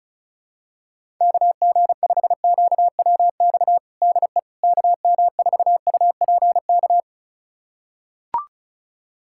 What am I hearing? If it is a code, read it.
Morse code, heard as a keyed tone, about 36 wpm, 700 Hz. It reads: KG5QWX DE KM4UPK
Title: USA Callsign DE Callsign - 6 Characters - 35wpm